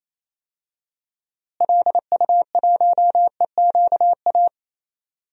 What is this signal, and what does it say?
Morse code, heard as a keyed tone, about 28 wpm, 700 Hz. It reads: LU1EQA
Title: DXCC-100 Call Signs - 6 Characters - Repeated 3x - 28wpm